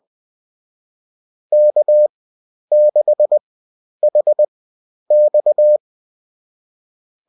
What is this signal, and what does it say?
Morse code, heard as a keyed tone, about 20 wpm, spaced as for 10 wpm, 600 Hz. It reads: K6HX